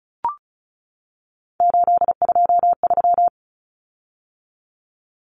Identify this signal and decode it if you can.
Morse code, heard as a keyed tone, about 34 wpm, 700 Hz. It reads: 823
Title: Common Cut Numbers - 3-Digit 35wpm